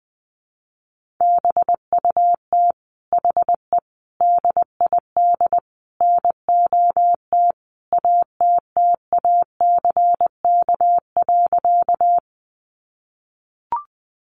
Morse code, heard as a keyed tone, about 20 wpm, 700 Hz. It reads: BUT HE DID NOT ATTACK.
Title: Sentences from Top 1500 Words 20wpm